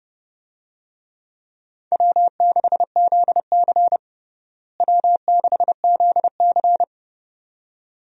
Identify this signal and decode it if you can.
Morse code, heard as a keyed tone, about 30 wpm, 700 Hz. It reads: W6ZC W6ZC
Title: USA Call Signs - 4 Characters - Repeated 2x - 30wpm